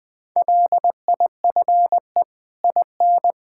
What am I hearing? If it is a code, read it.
Morse code, heard as a keyed tone, about 21 wpm, 700 Hz. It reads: LIFE IN
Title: Sets of 8 Words - 20wpm